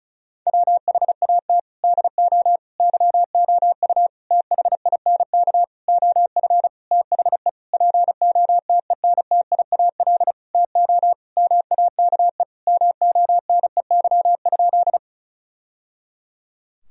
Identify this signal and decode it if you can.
Morse code, heard as a keyed tone, about 36 wpm, 700 Hz. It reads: WHAT DO YOU THINK OF THE POTENTIAL TO MAKE MONEY?
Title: Sentences from Top 1000 Words - Review Part 4 - 35wpm